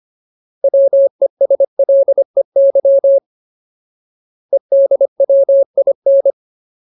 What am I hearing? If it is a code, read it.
Morse code, heard as a keyed tone, about 25 wpm, 550 Hz. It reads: WESLEY EDWIN
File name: Ultimate Word List / 25WPM Ultimate